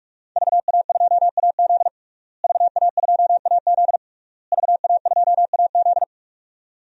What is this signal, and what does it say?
Morse code, heard as a keyed tone, about 45 wpm, 700 Hz. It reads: VA2AZ VA2AZ VA2AZ
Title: DXCC-100 Call Signs - 5 Characters - Repeated 3x - 45wpm